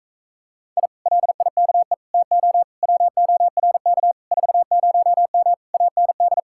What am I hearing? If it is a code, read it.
Morse code, heard as a keyed tone, about 45 wpm, 700 Hz. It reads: I LIKE TO WORK 40M AND